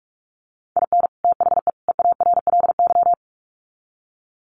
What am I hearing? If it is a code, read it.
Morse code, heard as a keyed tone, about 44 wpm, 700 Hz. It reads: IN THE EARLY
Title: Sets of 3 Words 45wpm